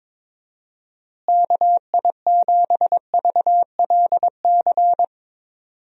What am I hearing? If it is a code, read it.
Morse code, heard as a keyed tone, about 21 wpm, 700 Hz. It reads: KI7VLC
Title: USA Call Signs - 6 Characters - Repeated 3x - 22wpm